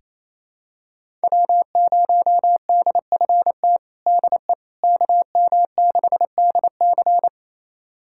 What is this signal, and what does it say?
Morse code, heard as a keyed tone, about 28 wpm, 700 Hz. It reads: W0DFT DE KM6DC
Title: USA Callsign DE Callsign - 5 Characters - 28wpm